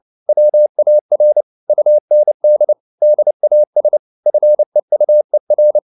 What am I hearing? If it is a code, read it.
Morse code, heard as a keyed tone, about 29 wpm, 600 Hz. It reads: WAR UND DAS FEUER